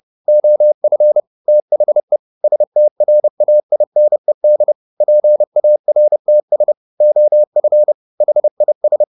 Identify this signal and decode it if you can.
Morse code, heard as a keyed tone, about 30 wpm, 600 Hz. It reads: OF THE STRAINED PARTS OF HIS